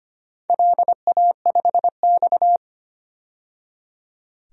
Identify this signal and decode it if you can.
Morse code, heard as a keyed tone, about 25 wpm, 700 Hz. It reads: LA5X
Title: DXCC-100 Call Signs - 4 Characters - 25wpm